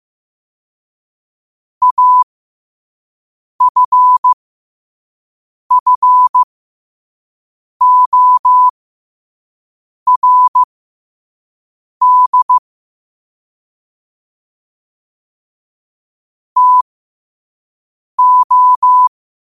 Morse code, heard as a keyed tone, about 15 wpm, spaced as for 5 wpm, 1000 Hz. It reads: AFFORD TO